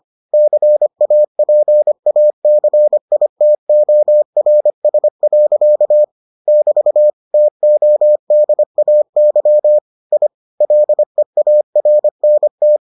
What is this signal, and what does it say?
Morse code, heard as a keyed tone, about 25 wpm, 600 Hz. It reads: CAPACITORS. <BT> TODAY I LEARNT